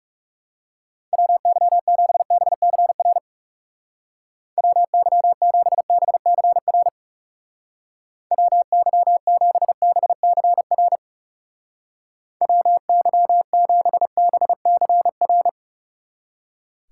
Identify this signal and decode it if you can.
Morse code, heard as a keyed tone, about 36 wpm, 700 Hz. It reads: WY7BCR WY7BCR WY7BCR WY7BCR
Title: Speed Racing - USA 2x3 Call Signs 30wpm